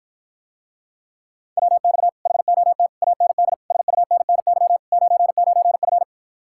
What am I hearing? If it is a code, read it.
Morse code, heard as a keyed tone, about 54 wpm, 700 Hz. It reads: WX HOT AND SUNNY 99F